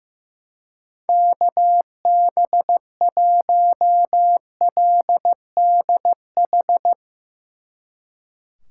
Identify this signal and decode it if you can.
Morse code, heard as a keyed tone, about 15 wpm, 700 Hz. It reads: KB1LDH
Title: Speed Racing - USA 2x3 Call Signs 15wpm